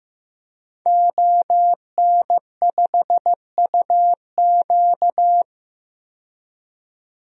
Morse code, heard as a keyed tone, about 15 wpm, 700 Hz. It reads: ON5UQ
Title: DXCC-100 Call Signs - 5 Characters - 15wpm